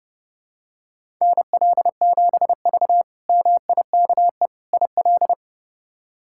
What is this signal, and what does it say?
Morse code, heard as a keyed tone, about 30 wpm, 700 Hz. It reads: NL7V MIKE IL
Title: K1USN SST QSO Practice - 30wpm